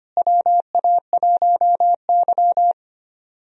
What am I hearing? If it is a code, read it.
Morse code, heard as a keyed tone, about 25 wpm, 700 Hz. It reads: WA1Y
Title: USA Call Signs - 4 Characters - Repeated 2x - 25wpm